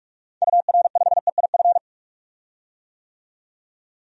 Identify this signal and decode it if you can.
Morse code, heard as a keyed tone, about 42 wpm, 700 Hz. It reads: UR5EIF
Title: DXCC-100 Call Signs - 6 Characters - 45wpm